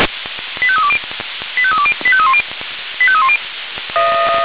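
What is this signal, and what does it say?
Five-tone selective call (ZVEI), heard as quick runs of five tones, tones of about 70 ms. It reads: 85320 85320 85310 85310
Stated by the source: ZVEI five-tone selective call received by SDR, from the Feuerwehr Orlamünde audio archive.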